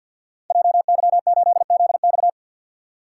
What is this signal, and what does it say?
Morse code, heard as a keyed tone, about 53 wpm, 700 Hz. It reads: JY8ZX
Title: Code Groups - 5 Letter:Numbers - 50wpm